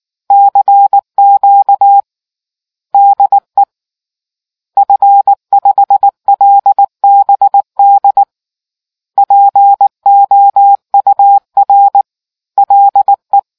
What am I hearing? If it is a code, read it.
Morse code, heard as a keyed tone, about 19 wpm, 800 Hz. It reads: CQ DE F5LBD POUR LE